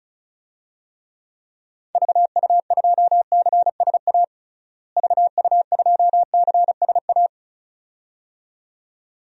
Morse code, heard as a keyed tone, about 36 wpm, 700 Hz. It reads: VU2CSA VU2CSA
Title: DXCC-100 Call Signs - 6 Characters - Repeated 2x - 35wpm